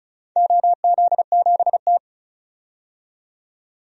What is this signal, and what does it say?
Morse code, heard as a keyed tone, about 36 wpm, 700 Hz. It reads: OZ7T
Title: DXCC-100 Call Signs - 4 Characters - 35wpm